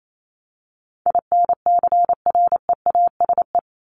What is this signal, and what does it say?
Morse code, heard as a keyed tone, about 28 wpm, 700 Hz. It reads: INCREASE
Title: LY Suffix - Top 100 Words with Base Words 28wpm